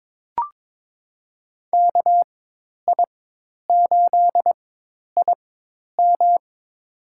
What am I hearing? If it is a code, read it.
Morse code, heard as a keyed tone, about 21 wpm, spaced as for 10 wpm, 700 Hz. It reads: KI8IM
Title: USA Call Signs - 5 Characters - 4x Character Spacing - 22wpm